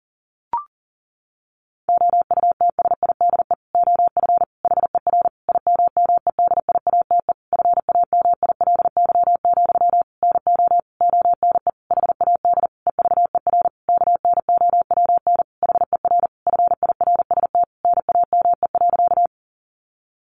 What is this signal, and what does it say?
Morse code, heard as a keyed tone, about 37 wpm, 700 Hz. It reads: OUTSIDE OF HER IMMEDIATE FAMILY, NO ONE HAD EVER KNOWN HER FIRST NAME.
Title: Sentences from Top 1200 Words 40wpm